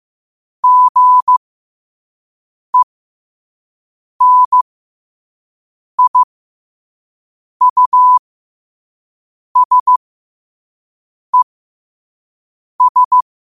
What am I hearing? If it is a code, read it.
Morse code, heard as a keyed tone, about 15 wpm, spaced as for 5 wpm, 1000 Hz. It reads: GENIUSES
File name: qotdfiles / 5wpm